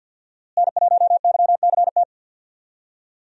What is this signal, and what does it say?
Morse code, heard as a keyed tone, about 51 wpm, 700 Hz. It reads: N1YCT
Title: USA Call Signs - 5 Characters - Repeated 3x - 50wpm